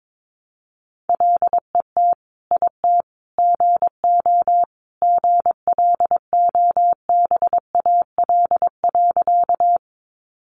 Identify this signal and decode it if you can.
Morse code, heard as a keyed tone, about 22 wpm, 700 Hz. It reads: LET IT GO GLOBAL.
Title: Sentences from Top 1200 Words 22wpm